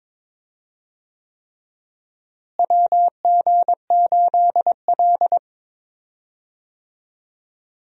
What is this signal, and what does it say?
Morse code, heard as a keyed tone, about 21 wpm, 700 Hz. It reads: WG8L
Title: USA Call Signs - 4 Characters - Repeated 3x - 22wpm